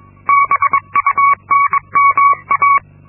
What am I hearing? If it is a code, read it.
Morse code, heard as a keyed tone, about 22 wpm, 1120 Hz. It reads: BUNMA
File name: Linksmakalnis-2